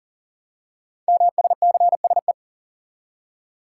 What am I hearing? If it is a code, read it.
Morse code, heard as a keyed tone, about 40 wpm, 690 Hz. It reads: MSCSE
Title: Code Groups - 5 Letter:Numbers - 40wpm